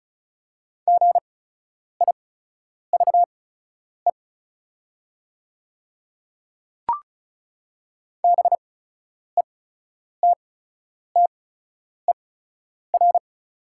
Morse code, heard as a keyed tone, about 36 wpm, spaced as for 10 wpm, 700 Hz. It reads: GIVE BETTER
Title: 8x Character Spacing - Top 500 Words - 35wpm